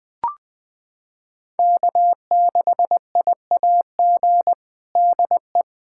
Morse code, heard as a keyed tone, about 20 wpm, 700 Hz. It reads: K6IAG DE
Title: USA Callsign DE Callsign - 5 Characters - 20wpm